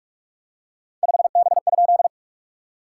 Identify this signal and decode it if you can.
Morse code, heard as a keyed tone, about 44 wpm, 660 Hz. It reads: 5B?
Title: SOTA QSO Practice 45wpm